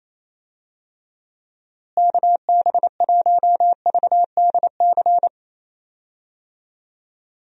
Morse code, heard as a keyed tone, about 28 wpm, 700 Hz. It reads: KB1VDC
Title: USA Call Signs - 6 Characters - Repeated 3x - 28wpm